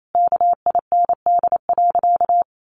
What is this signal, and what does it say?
Morse code, heard as a keyed tone, about 28 wpm, 700 Hz. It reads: KIND.
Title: Sentences from Top 500 Words 28wpm